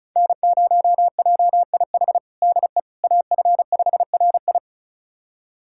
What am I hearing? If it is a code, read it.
Morse code, heard as a keyed tone, about 36 wpm, 700 Hz. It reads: N0JIH DE AF5RI
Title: USA Callsign DE Callsign - 5 Characters - 35wpm